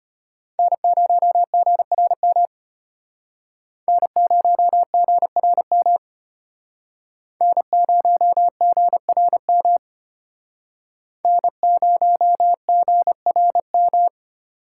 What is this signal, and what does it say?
Morse code, heard as a keyed tone, about 34 wpm, 700 Hz. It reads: N0GRM N0GRM N0GRM N0GRM